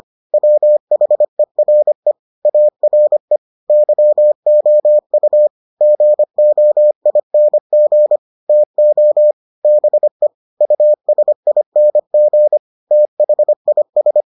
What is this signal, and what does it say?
Morse code, heard as a keyed tone, about 25 wpm, 600 Hz. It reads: WHERE ARE YOU GOING TO BE USING THIS